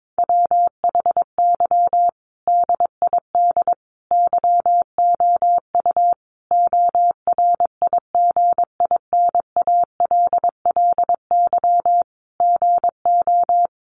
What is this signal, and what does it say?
Morse code, heard as a keyed tone, about 22 wpm, 700 Hz. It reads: WHY DID YOU ORIGINALLY GO
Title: Sentences from Top 1500 Words 22wpm